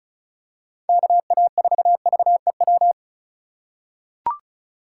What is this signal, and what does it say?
Morse code, heard as a keyed tone, about 34 wpm, 700 Hz. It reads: KA4VEW
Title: USA Call Signs - 3x Character Spacing - 35wpm